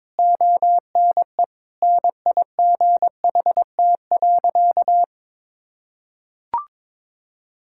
Morse code, heard as a keyed tone, about 22 wpm, 700 Hz. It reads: ONE NIGHT.
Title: Sentences from Top 1600 Words 22wpm